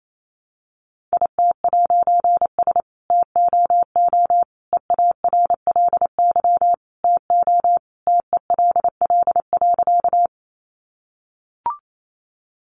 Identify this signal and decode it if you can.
Morse code, heard as a keyed tone, about 28 wpm, 700 Hz. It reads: IT'S TOO EARLY TO TELL.
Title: A Sound Of Thunder 28wpm